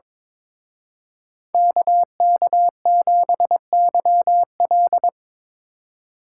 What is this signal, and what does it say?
Morse code, heard as a keyed tone, about 25 wpm, 700 Hz. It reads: KK7YL
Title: Speed Racing - USA 2x2 Call Signs 22wpm